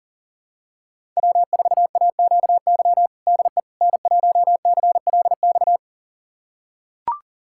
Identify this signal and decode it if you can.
Morse code, heard as a keyed tone, about 40 wpm, 700 Hz. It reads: W4AQY DE N1CLX
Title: USA Callsign DE Callsign - 5 Characters - 40wpm